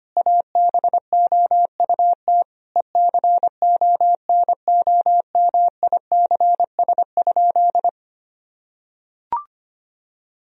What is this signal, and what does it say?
Morse code, heard as a keyed tone, about 25 wpm, 700 Hz. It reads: ABOUT ECONOMICS?
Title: Sentences from Top 1000 Words - Review Part 3 - 25wpm